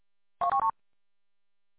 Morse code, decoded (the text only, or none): none